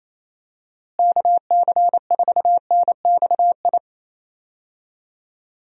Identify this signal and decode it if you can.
Morse code, heard as a keyed tone, about 28 wpm, 700 Hz. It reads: KC4NXI
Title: USA Call Signs - 6 Characters - Repeated 2x - 28wpm